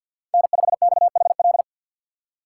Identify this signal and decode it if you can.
Morse code, heard as a keyed tone, about 52 wpm, 700 Hz. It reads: N5XHL
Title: USA Call Signs - 4x Character Spacing - 50wpm